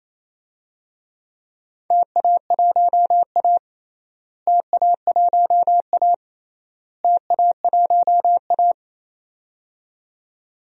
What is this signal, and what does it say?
Morse code, heard as a keyed tone, about 28 wpm, 700 Hz. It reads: TA1A TA1A TA1A
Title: DXCC-100 Call Signs - 4 Characters - Repeated 3x - 28wpm